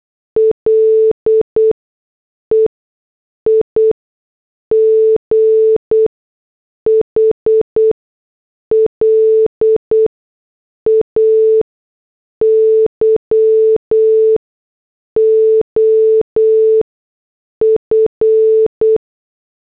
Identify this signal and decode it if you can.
Morse code, heard as a keyed tone, about 8 wpm, spaced as for 6 wpm, 440 Hz. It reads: LEIGH LAYOF
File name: morse / headlines